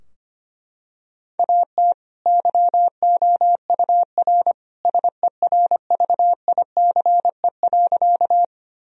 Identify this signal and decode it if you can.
Morse code, heard as a keyed tone, about 25 wpm, 700 Hz. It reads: AT YOUR SERVICE.
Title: Sentences from Top 1000 Words - Review Part 4 - 25wpm